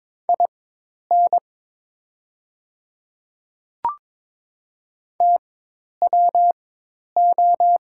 Morse code, heard as a keyed tone, about 22 wpm, spaced as for 10 wpm, 700 Hz. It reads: IN TWO
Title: 4x Character Spacing - Top 100 Words - 22wpm